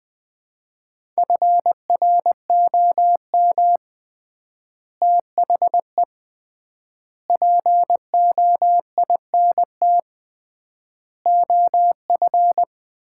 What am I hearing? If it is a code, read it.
Morse code, heard as a keyed tone, about 21 wpm, 700 Hz. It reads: FROM THE POINT OF
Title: 3x Word-Spacing - Sets of 5 Words - 20wpm